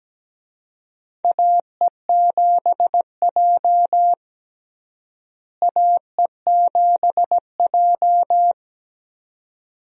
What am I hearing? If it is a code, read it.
Morse code, heard as a keyed tone, about 17 wpm, 700 Hz. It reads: AE7J AE7J
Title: USA Call Signs - 4 Characters - Repeated 2x - 17wpm